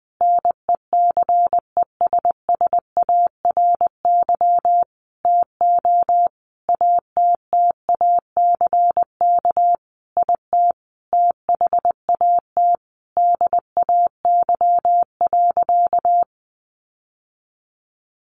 Morse code, heard as a keyed tone, about 20 wpm, 700 Hz. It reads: NECESSARY TO ATTACK IT THAT DAY.
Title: Sentences from Top 1500 Words 20wpm